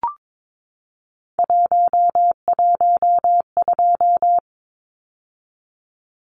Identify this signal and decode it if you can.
Morse code, heard as a keyed tone, about 22 wpm, 700 Hz. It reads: 112
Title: Numbers - 3-Digit 22wpm